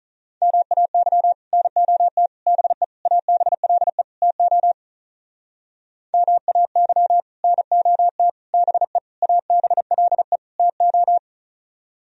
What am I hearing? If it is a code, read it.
Morse code, heard as a keyed tone, about 46 wpm, 700 Hz. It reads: MAY NOT BE ABLE TO MAY NOT BE ABLE TO